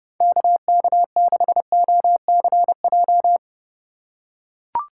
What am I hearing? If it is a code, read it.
Morse code, heard as a keyed tone, about 30 wpm, 700 Hz. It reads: KK6OCJ